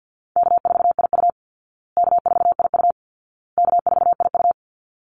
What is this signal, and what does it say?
Morse code, heard as a keyed tone, about 48 wpm, 700 Hz. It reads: K4IU K4IU K4IU
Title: Easy Call Signs Repeated 3x - 50wpm